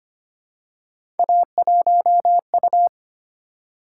Morse code, heard as a keyed tone, about 25 wpm, 700 Hz. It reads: A1U
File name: Three Letter-Number Combinations - 25wpm